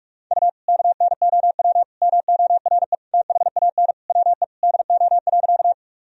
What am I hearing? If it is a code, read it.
Morse code, heard as a keyed tone, about 42 wpm, 700 Hz. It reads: U KNOW MORE THAN WE DO.
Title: Sentences from Top 100 Words and Abbreviations 45wpm